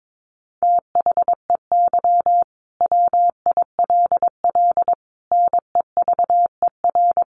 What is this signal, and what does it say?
Morse code, heard as a keyed tone, about 22 wpm, 700 Hz. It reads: THEY WILL NEVER
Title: Sentences from Top 1300 Words 22wpm